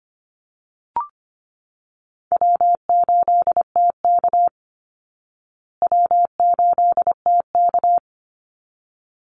Morse code, heard as a keyed tone, about 25 wpm, 700 Hz. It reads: W8TK W8TK
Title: Call Signs Repeated 3x - 25wpm